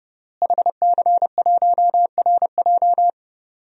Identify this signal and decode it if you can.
Morse code, heard as a keyed tone, about 32 wpm, 700 Hz. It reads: HC1RJ
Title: DXCC-100 Call Signs - 5 Characters - Repeated 3x - 30wpm